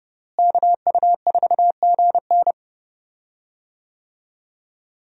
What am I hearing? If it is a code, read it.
Morse code, heard as a keyed tone, about 30 wpm, 700 Hz. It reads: KU4GN
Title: USA Call Signs - 5 Characters - Repeated 3x - 30wpm